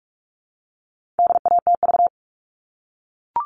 Morse code, heard as a keyed tone, about 44 wpm, 700 Hz. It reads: DATV
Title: All Cut Numbers - 4-Digit 45wpm